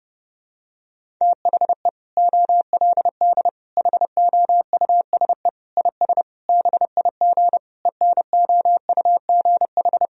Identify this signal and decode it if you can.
Morse code, heard as a keyed tone, about 30 wpm, 700 Hz. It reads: THE OLD HOUSE IS BIG ENOUGH
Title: Sentences from Top 1000 Words - Review Part 1 - 30wpm